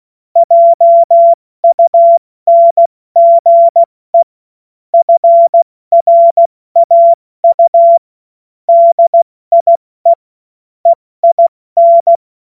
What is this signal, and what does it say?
Morse code, heard as a keyed tone, about 16 wpm, spaced as for 14 wpm, 680 Hz. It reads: JUNGE FRAU DIE EIN
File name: Morsekurs / Witze / Witz 05 Dezember 2023 70 F80